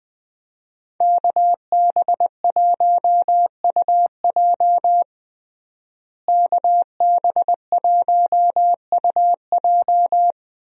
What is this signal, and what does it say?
Morse code, heard as a keyed tone, about 21 wpm, 700 Hz. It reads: KB1UJ KB1UJ
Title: USA Call Signs - 5 Characters - Repeated 3x - 20wpm